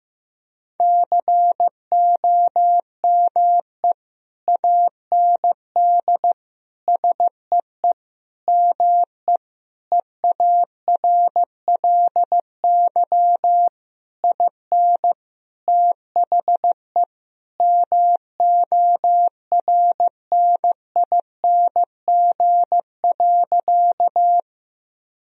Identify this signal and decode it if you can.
Morse code, heard as a keyed tone, about 15 wpm, 700 Hz. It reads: COME AND SEE ME EARLY IN THE MORNING.
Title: Sentences from Top 1000 Words - Review Part 2 - 15wpm